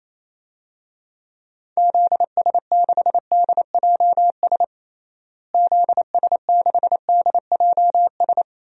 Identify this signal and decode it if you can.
Morse code, heard as a keyed tone, about 28 wpm, 700 Hz. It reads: ZS6DJS ZS6DJS
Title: DXCC-100 Call Signs - 6 Characters - Repeated 2x - 28wpm